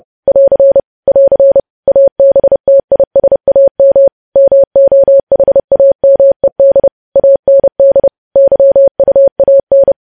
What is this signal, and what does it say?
Morse code, heard as a keyed tone, about 30 wpm, 550 Hz. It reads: <AR> <AR> ABTISAM MOHAMED AND YUAN
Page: news headlines in morse code at 30 wpm